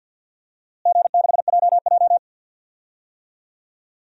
Morse code, heard as a keyed tone, about 52 wpm, 700 Hz. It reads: G6JJ
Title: DXCC-100 Call Signs - 4 Characters - Repeated 2x - 50wpm